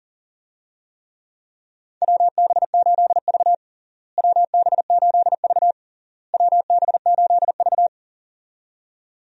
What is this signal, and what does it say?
Morse code, heard as a keyed tone, about 40 wpm, 700 Hz. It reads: WB8V WB8V WB8V
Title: USA Call Signs - 4 Characters - Repeated 3x - 40wpm